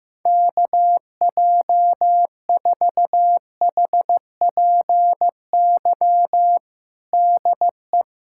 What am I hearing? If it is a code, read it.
Morse code, heard as a keyed tone, about 16 wpm, 700 Hz. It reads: KJ4HPY DE
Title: USA Callsign DE Callsign 15wpm